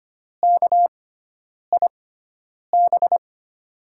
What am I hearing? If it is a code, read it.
Morse code, heard as a keyed tone, about 25 wpm, spaced as for 9 wpm, 700 Hz. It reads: KIB
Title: USA Call Signs - 6 Characters - 6x Character Spacing - 25wpm